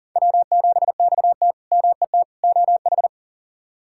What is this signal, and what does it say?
Morse code, heard as a keyed tone, about 40 wpm, 700 Hz. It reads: W7XT MET OH
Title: K1USN SST QSO Practice - 40wpm